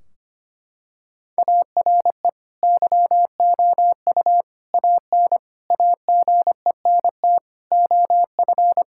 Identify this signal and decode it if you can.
Morse code, heard as a keyed tone, about 25 wpm, 700 Hz. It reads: ARE YOU AN AGENT OF
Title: Sentences from Top 1600 Words 25wpm